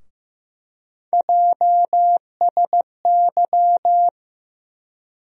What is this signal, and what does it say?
Morse code, heard as a keyed tone, about 15 wpm, 700 Hz. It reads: JSY
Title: USA Call Sign Postfixes - 3 Letters 15wpm